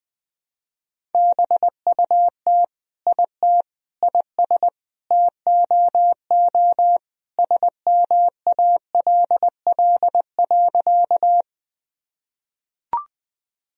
Morse code, heard as a keyed tone, about 20 wpm, 700 Hz. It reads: BUT IT IS TOO SMALL.
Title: Sentences from Top 1000 Words - Review Part 1 - 20wpm